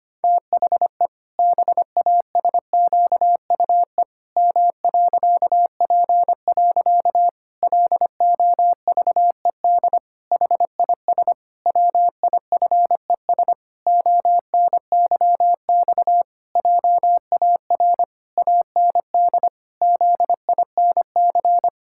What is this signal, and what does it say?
Morse code, heard as a keyed tone, about 25 wpm, 700 Hz. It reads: THE BASQUE M.P. LOVED HIS WIFES ONYX JAR AND ZINC
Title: Pangram Challenge v2 - 25wpm